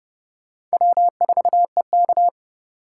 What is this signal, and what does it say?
Morse code, heard as a keyed tone, about 38 wpm, 700 Hz. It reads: W4EK